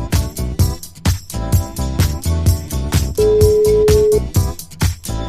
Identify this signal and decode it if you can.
Ringback tone (Europe, 425 Hz).